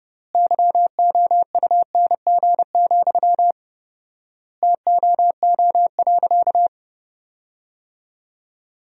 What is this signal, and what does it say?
Morse code, heard as a keyed tone, about 32 wpm, 700 Hz. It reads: YOUNG, TOO.
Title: 4x Word-Spacing - Sentences from Top 400 Words - 30wpm